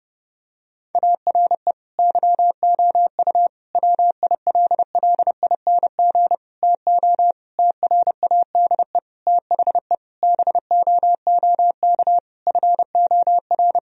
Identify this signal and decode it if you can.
Morse code, heard as a keyed tone, about 32 wpm, 700 Hz. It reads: ARE YOU WILLING TO TRADE THE BOOK FOR
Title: Sentences from Top 1000 Words - Review Part 4 - 30wpm